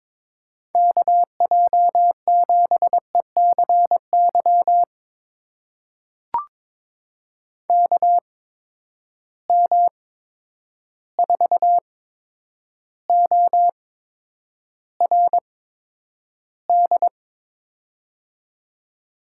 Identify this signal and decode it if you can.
Morse code, heard as a keyed tone, about 22 wpm, spaced as for 9 wpm, 700 Hz. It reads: KJ7ECY KM4ORD
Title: USA Call Signs - 6 Characters - 8x Character Spacing - 22wpm